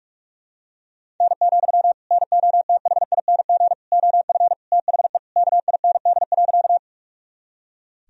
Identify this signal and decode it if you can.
Morse code, heard as a keyed tone, about 42 wpm, 690 Hz. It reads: N, NOTHING OF THE KIND.
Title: Sentences from Top 500 Words and Abbreviations 45wpm